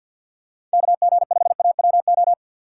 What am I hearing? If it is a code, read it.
Morse code, heard as a keyed tone, about 50 wpm, 700 Hz. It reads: KG5AWO
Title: USA Call Signs - 3x Character Spacing - 50wpm